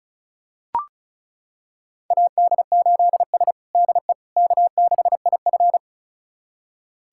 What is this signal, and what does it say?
Morse code, heard as a keyed tone, about 36 wpm, 700 Hz. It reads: AD8S DE K6IF
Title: USA Callsign DE Callsign - 4 Characters - 35wpm